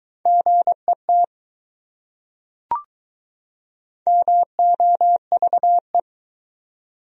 Morse code, heard as a keyed tone, about 22 wpm, 700 Hz. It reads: GET MOVE